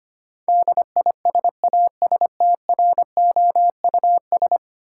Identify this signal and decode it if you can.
Morse code, heard as a keyed tone, about 25 wpm, 700 Hz. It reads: DISASTROUS
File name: 10 Letter Words - First 3 Letters Given - 25wpm